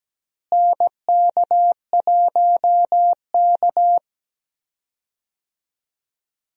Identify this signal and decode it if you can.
Morse code, heard as a keyed tone, about 17 wpm, 700 Hz. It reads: NK1K
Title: USA Call Signs - 4 Characters - Repeated 2x - 17wpm